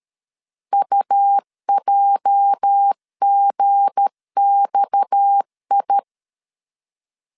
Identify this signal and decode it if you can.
Morse code, heard as a keyed tone, about 13 wpm, 800 Hz. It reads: UJGXI
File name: LEZIONI / 8 YJ45 / 8 YJ 12wpm 2235